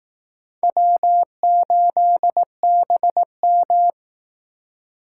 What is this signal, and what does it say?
Morse code, heard as a keyed tone, about 18 wpm, 700 Hz. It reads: W8BM
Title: Speed Racing - USA 1x2 Call Signs 15wpm